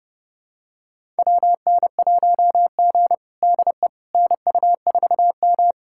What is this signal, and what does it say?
Morse code, heard as a keyed tone, about 30 wpm, 700 Hz. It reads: WN1G DE NU4M
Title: USA Callsign DE Callsign - 4 Characters - 30wpm